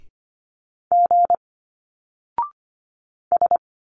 Morse code, heard as a keyed tone, about 25 wpm, 700 Hz. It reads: G S
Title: Single Letters 25wpm